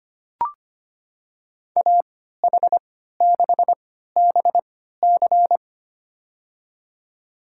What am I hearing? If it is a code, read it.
Morse code, heard as a keyed tone, about 25 wpm, spaced as for 14 wpm, 700 Hz. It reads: AH6BC
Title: USA Call Signs - 5 Characters - 3x Character Spacing - 25wpm